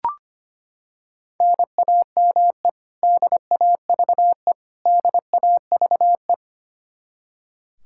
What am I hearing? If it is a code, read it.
Morse code, heard as a keyed tone, about 25 wpm, 700 Hz. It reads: NAME DAVE DAVE
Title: Practice QSO - Exchange 1 - Beginning - 25wpm